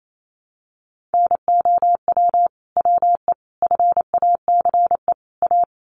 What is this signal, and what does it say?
Morse code, heard as a keyed tone, about 28 wpm, 700 Hz. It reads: NOW WE FACE A
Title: Sentences from Top 1200 Words 28wpm